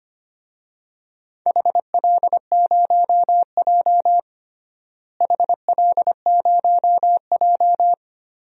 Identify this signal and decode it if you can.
Morse code, heard as a keyed tone, about 25 wpm, 700 Hz. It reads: HL0J HL0J
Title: DXCC-100 Call Signs - 4 Characters - Repeated 3x - 25wpm